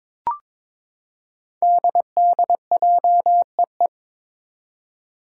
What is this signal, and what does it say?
Morse code, heard as a keyed tone, about 22 wpm, 700 Hz. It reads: DDJEE
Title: Five Challenging Letter-Number Combinations 22wpm